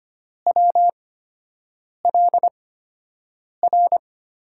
Morse code, heard as a keyed tone, about 25 wpm, spaced as for 7 wpm, 700 Hz. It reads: WLR
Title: USA Call Signs - 5 Characters - 8x Character Spacing - 25wpm